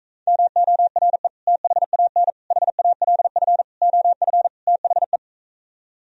Morse code, heard as a keyed tone, about 46 wpm, 700 Hz. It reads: MORE THAN HALF OF THE